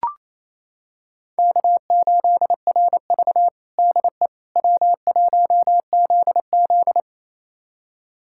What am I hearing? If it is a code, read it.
Morse code, heard as a keyed tone, about 28 wpm, 700 Hz. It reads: K8RV DE W1ZZ
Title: USA Callsign DE Callsign - 4 Characters - 28wpm